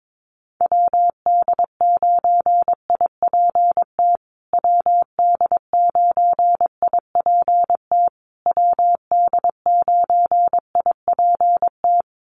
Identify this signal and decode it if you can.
Morse code, heard as a keyed tone, about 22 wpm, 690 Hz. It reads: WD9IPT WD9IPT WD9IPT
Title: USA Call Signs Repeated 3x 22wpm